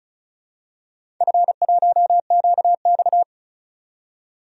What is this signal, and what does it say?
Morse code, heard as a keyed tone, about 36 wpm, 700 Hz. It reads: F1QX
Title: DXCC-100 Call Signs - 4 Characters - 35wpm